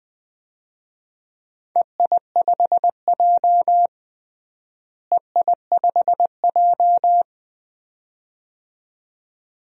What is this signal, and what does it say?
Morse code, heard as a keyed tone, about 20 wpm, 700 Hz. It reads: EI5J EI5J
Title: DXCC-100 Call Signs - 4 Characters - Repeated 2x - 20wpm